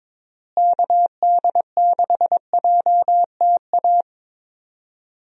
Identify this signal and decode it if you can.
Morse code, heard as a keyed tone, about 22 wpm, 700 Hz. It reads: KD6JTA
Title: USA Call Signs - 6 Characters - Repeated 3x - 22wpm